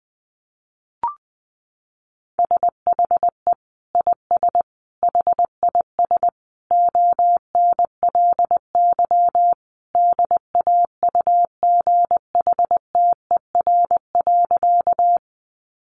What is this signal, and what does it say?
Morse code, heard as a keyed tone, about 20 wpm, 700 Hz. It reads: SHE IS HIS ONLY DAUGHTER.
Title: Sentences from Top 1500 Words 20wpm